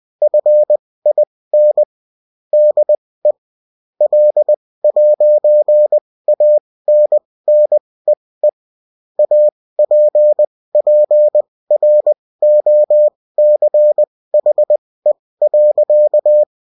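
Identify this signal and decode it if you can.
Morse code, heard as a keyed tone, about 20 wpm, spaced as for 16 wpm, 600 Hz. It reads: FIN DE L'ANNEE APPROCHE.
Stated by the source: Bulletin in text and CW form.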